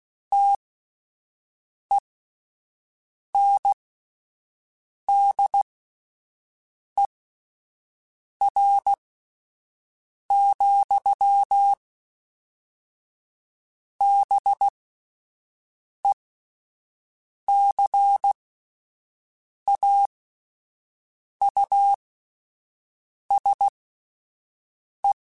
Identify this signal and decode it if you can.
Morse code, heard as a keyed tone, about 16 wpm, spaced as for 5 wpm, 800 Hz. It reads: TENDER, BECAUSE
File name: Spaced / Chuckles Spaced-mp3s / SoftWords sp @10